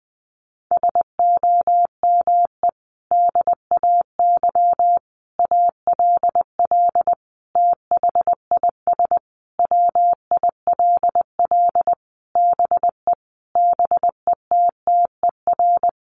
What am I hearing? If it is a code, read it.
Morse code, heard as a keyed tone, about 20 wpm, 700 Hz. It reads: SOME DAY ALL THIS WILL BE BETTER
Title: Sentences from Top 1300 Words 20wpm